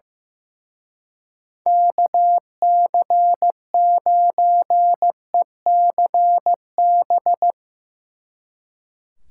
Morse code, heard as a keyed tone, about 16 wpm, 700 Hz. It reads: KC9ECB
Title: Speed Racing - USA 2x3 Call Signs 15wpm